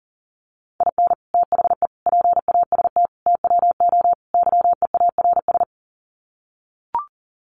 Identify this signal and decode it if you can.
Morse code, heard as a keyed tone, about 40 wpm, 700 Hz. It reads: IN THE PAST TWO YEARS
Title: Sets of 5 Words 40wpm